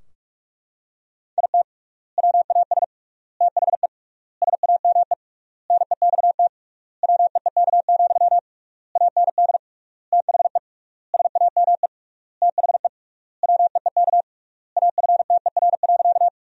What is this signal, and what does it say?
Morse code, heard as a keyed tone, about 42 wpm, 700 Hz. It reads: IT WAS THE SAME NEXT WEEK, AND THE SAME THE WEEK AFTER.
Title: 3x Word-Spacing - Sentences from Top 500 Words - 45wpm